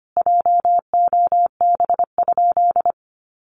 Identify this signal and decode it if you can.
Morse code, heard as a keyed tone, about 25 wpm, 700 Hz. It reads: JOB?
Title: Sentences from Top 1300 Words 25wpm